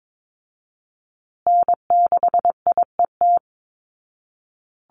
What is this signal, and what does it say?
Morse code, heard as a keyed tone, about 22 wpm, 700 Hz. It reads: N6IET
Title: Call Signs Repeated 2x - 22wpm